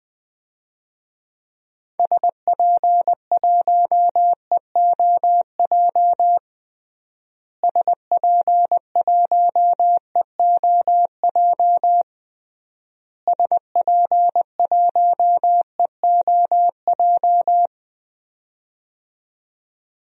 Morse code, heard as a keyed tone, about 20 wpm, 700 Hz. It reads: SP1EOJ SP1EOJ SP1EOJ
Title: DXCC-100 Call Signs - 6 Characters - Repeated 3x - 20wpm